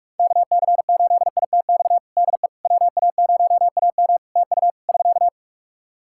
Morse code, heard as a keyed tone, about 47 wpm, 700 Hz. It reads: KC8ITX DE WA0AM TU <SK>
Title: Practice QSO - Exchange 4 - Ending - 45wpm